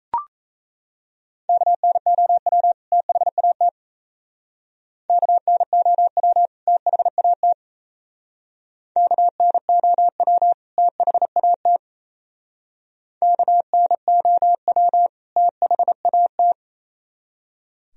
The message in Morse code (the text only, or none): KNOW THAT KNOW THAT KNOW THAT KNOW THAT